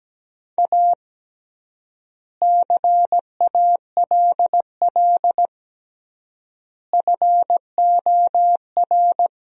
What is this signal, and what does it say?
Morse code, heard as a keyed tone, about 17 wpm, 700 Hz. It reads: A CALL FOR
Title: 3x Word-Spacing - Sentences from Top 300 Words - 17wpm